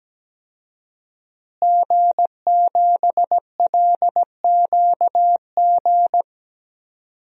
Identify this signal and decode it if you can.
Morse code, heard as a keyed tone, about 17 wpm, 700 Hz. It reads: G7LQG
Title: DXCC-100 Call Signs - 5 Characters - Repeated 2x - 17wpm